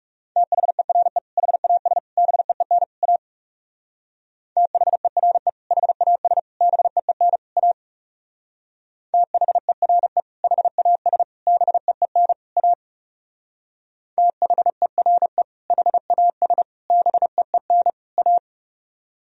Morse code, heard as a keyed tone, about 38 wpm, 700 Hz. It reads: THERE HAS BEEN A THERE HAS BEEN A THERE HAS BEEN A THERE HAS BEEN A
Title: Speed-Racing - Sets of 4 Words 30wpm